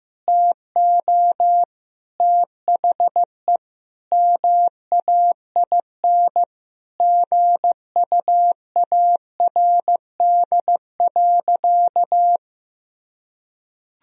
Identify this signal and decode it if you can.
Morse code, heard as a keyed tone, about 15 wpm, 700 Hz. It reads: TO THE MAIN GUARD.
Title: Sentences from Top 1000 Words - Review Part 4 - 15wpm